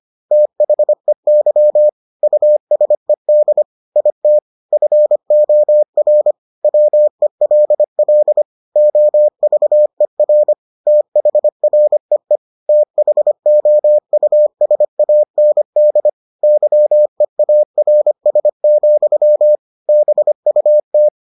Morse code, heard as a keyed tone, about 25 wpm, 600 Hz. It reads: THEY USED IT FOR WELL OVER THREE THOUSAND YEARS, BUT